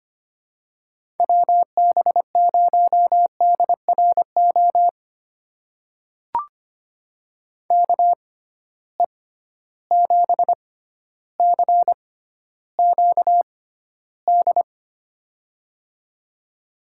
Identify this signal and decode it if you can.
Morse code, heard as a keyed tone, about 25 wpm, spaced as for 13 wpm, 700 Hz. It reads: WB0DRO KE7CQD
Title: USA Call Signs - 6 Characters - 6x Character Spacing - 25wpm